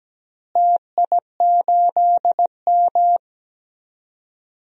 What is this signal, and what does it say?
Morse code, heard as a keyed tone, about 17 wpm, 700 Hz. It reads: TI8M
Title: DXCC-100 Call Signs - 4 Characters - Repeated 3x - 17wpm